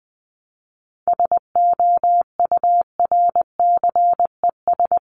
Morse code, heard as a keyed tone, about 20 wpm, 700 Hz. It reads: SOURCES
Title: ES:S Suffix - Top 100 Words with Base Words 20wpm